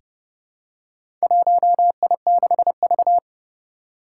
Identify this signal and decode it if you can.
Morse code, heard as a keyed tone, about 30 wpm, 700 Hz. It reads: 1I6V
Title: Four Challenging Letter-Number Combinations 30wpm